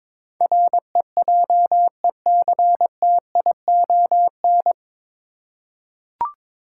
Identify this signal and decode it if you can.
Morse code, heard as a keyed tone, about 22 wpm, 700 Hz. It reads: REJECTION
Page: Morse Code Ninja practice 22wpm